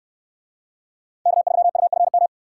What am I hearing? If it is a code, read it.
Morse code, heard as a keyed tone, about 68 wpm, 700 Hz. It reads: K3UFM